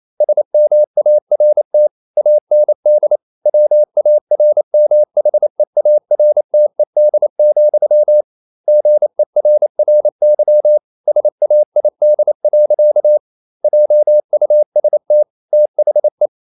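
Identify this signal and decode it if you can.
Morse code, heard as a keyed tone, about 28 wpm, 600 Hz. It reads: SMART AND WARMHEARTED, GERRY SAID. JUST THE
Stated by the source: Morse code practice at 28 wpm, text of Ken Follett's novel Never in English, from CW med Gnister.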